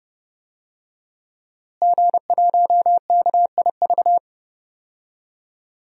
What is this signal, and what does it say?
Morse code, heard as a keyed tone, about 30 wpm, 700 Hz. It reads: G1KIV
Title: DXCC-100 Call Signs - 5 Characters - Repeated 2x - 30wpm